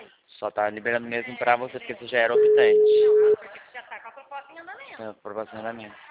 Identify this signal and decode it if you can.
Ringback tone (Europe, 425 Hz), with speech.